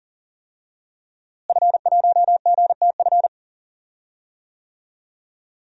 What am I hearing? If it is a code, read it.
Morse code, heard as a keyed tone, about 38 wpm, 700 Hz. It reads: F1GTF
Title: DXCC-100 Call Signs - 5 Characters - Repeated 2x - 40wpm